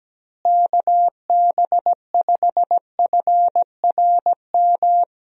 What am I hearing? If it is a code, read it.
Morse code, heard as a keyed tone, about 17 wpm, 700 Hz. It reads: KB5FRM
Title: USA Call Signs - 6 Characters - Repeated 2x - 17wpm